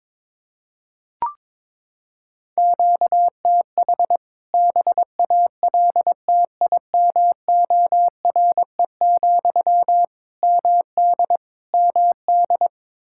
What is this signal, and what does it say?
Morse code, heard as a keyed tone, about 21 wpm, 700 Hz. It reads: QTH BALTIMORE, MD MD
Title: K1USN SST QSO Practice 22wpm